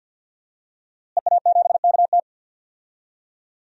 Morse code, heard as a keyed tone, about 51 wpm, 700 Hz. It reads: EA7KT
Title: DXCC-100 Call Signs - 5 Characters - Repeated 3x - 50wpm